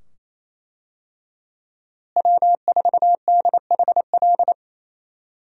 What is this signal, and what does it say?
Morse code, heard as a keyed tone, about 28 wpm, 700 Hz. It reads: W4DHL
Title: USA Call Signs - 5 Characters - Repeated 2x - 28wpm